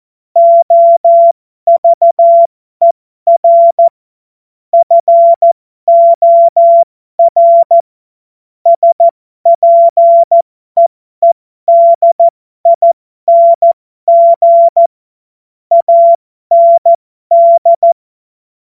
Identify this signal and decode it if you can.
Morse code, heard as a keyed tone, about 14 wpm, spaced as for 12 wpm, 680 Hz. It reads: OVER FOR SPEEDING AND
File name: Morsekurs / Witze